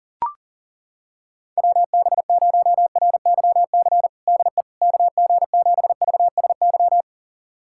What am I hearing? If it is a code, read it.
Morse code, heard as a keyed tone, about 40 wpm, 700 Hz. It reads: WB0RYC DE KG7VSY
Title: USA Callsign DE Callsign 40wpm